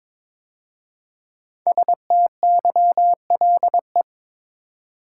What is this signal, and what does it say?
Morse code, heard as a keyed tone, about 22 wpm, 690 Hz. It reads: STYLE
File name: 5 Letter Words - First 3 Letters Given - 22wpm